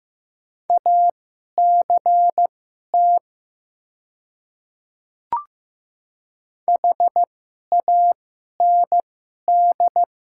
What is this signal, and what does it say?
Morse code, heard as a keyed tone, about 15 wpm, spaced as for 11 wpm, 700 Hz. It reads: ACT HAND
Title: 2x Character Spacing - Top 300 Words - 15wpm